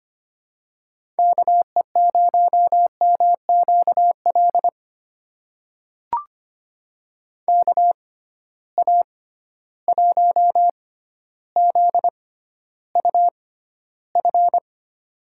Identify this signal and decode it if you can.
Morse code, heard as a keyed tone, about 25 wpm, spaced as for 13 wpm, 700 Hz. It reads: KE0MQL KA1ZUF
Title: USA Call Signs - 6 Characters - 6x Character Spacing - 25wpm